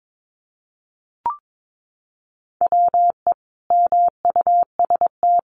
Morse code, heard as a keyed tone, about 22 wpm, 700 Hz. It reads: WE MUST
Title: Sentences from Top 1300 Words 22wpm